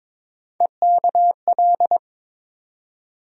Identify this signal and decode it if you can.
Morse code, heard as a keyed tone, about 22 wpm, 700 Hz. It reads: EKL
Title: USA Call Sign Postfixes - 3 Letters 22wpm